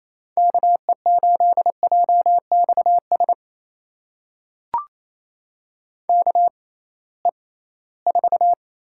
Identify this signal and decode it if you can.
Morse code, heard as a keyed tone, about 28 wpm, spaced as for 18 wpm, 700 Hz. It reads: KE8JXS KE4
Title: USA Call Signs - 6 Characters - 6x Character Spacing - 28wpm